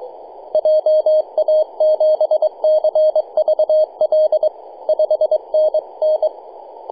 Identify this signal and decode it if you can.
Morse code, heard as a keyed tone, about 22 wpm, 620 Hz, with radio noise. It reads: JA7CVL 5NN
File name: posts / 2015-sept-13-0732z-nh2b-kh0-18-082-cw